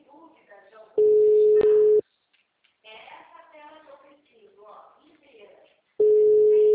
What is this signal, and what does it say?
Ringback tone (Europe, 425 Hz), with speech.
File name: out- -2016- - - .9